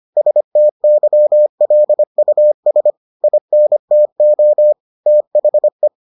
Morse code, heard as a keyed tone, about 25 wpm, 600 Hz. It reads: STYLUS INTO THE